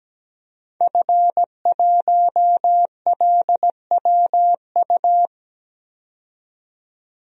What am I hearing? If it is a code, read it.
Morse code, heard as a keyed tone, about 17 wpm, 700 Hz. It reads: F1LWU
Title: DXCC-100 Call Signs - 5 Characters - 17wpm